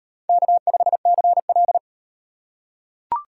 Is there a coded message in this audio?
Yes: Morse code from 0.3 to 1.8 s, heard as a keyed tone, about 38 wpm, 700 Hz. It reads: K5CL